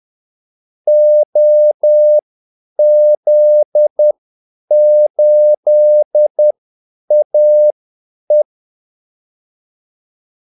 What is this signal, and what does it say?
Morse code, heard as a keyed tone, about 10 wpm, spaced as for 8 wpm, 600 Hz. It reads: OZ8AE
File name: Callsigns-10-8wpm